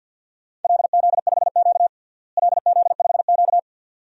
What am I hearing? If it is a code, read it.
Morse code, heard as a keyed tone, about 51 wpm, 700 Hz. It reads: LZ5Q LZ5Q
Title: DXCC-100 Call Signs - 4 Characters - Repeated 2x - 50wpm